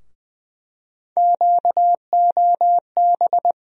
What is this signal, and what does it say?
Morse code, heard as a keyed tone, about 20 wpm, 700 Hz. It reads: QOB
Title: USA Call Sign Postfixes - 3 Letters 20wpm